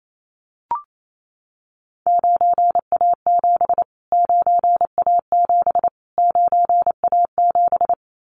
Morse code, heard as a keyed tone, about 28 wpm, 700 Hz. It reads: 9A7 9A7 9A7
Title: Callsign Prefixes for Top 25 DXCC - Repeated 3x - 28wpm